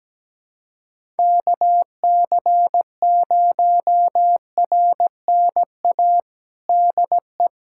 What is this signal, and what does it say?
Morse code, heard as a keyed tone, about 17 wpm, 700 Hz. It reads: KC0RNA DE
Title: USA Callsign DE Callsign 17wpm